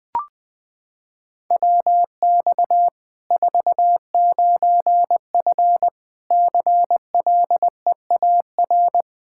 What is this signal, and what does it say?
Morse code, heard as a keyed tone, about 21 wpm, 700 Hz. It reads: WX 49F CLEAR
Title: Practice QSO - Exchange 2 - Info - 20wpm